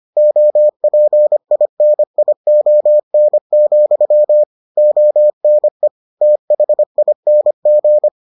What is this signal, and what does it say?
Morse code, heard as a keyed tone, about 25 wpm, 600 Hz. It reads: OPINION, ONE THING